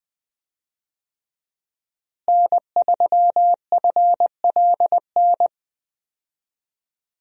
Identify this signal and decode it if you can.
Morse code, heard as a keyed tone, about 20 wpm, 700 Hz. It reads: N3FLN
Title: USA Call Signs - 5 Characters - Repeated 2x - 20wpm